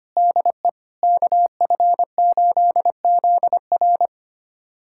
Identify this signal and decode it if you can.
Morse code, heard as a keyed tone, about 25 wpm, 700 Hz. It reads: DE KF8ZR
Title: USA Callsign DE Callsign - 5 Characters - 25wpm